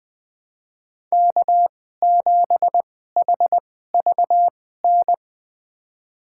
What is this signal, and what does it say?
Morse code, heard as a keyed tone, about 20 wpm, spaced as for 14 wpm, 700 Hz. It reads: K7HVN
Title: USA Call Signs - 2x Character Spacing - 20wpm